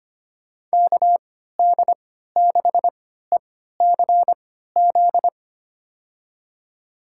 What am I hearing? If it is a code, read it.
Morse code, heard as a keyed tone, about 25 wpm, spaced as for 14 wpm, 700 Hz. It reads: KD6ECZ
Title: USA Call Signs - 6 Characters - 3x Character Spacing - 25wpm